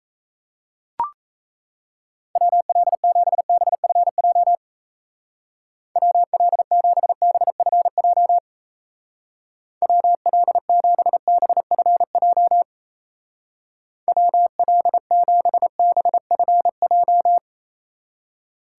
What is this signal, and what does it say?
Morse code, heard as a keyed tone, about 35 wpm, 700 Hz. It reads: WL7BFJ WL7BFJ WL7BFJ WL7BFJ
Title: Speed Racing - USA 2x3 Call Signs 28wpm